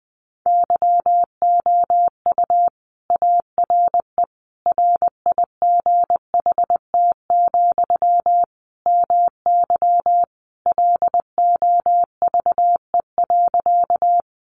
Morse code, heard as a keyed tone, about 20 wpm, 700 Hz. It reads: YOU ARE RIGHT, MY LOVE.
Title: Sentences from Top 400 Words 20wpm